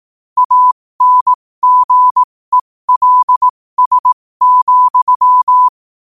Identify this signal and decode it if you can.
Morse code, heard as a keyed tone, about 18 wpm, spaced as for 15 wpm, 1000 Hz. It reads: ANGELS,
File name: qotdfiles / 15wpm